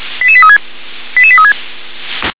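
Five-tone selective call (ZVEI), heard as quick runs of five tones, tones of about 80 ms. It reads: 88836 88836
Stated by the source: ZVEI five-tone selective call received on an SDR.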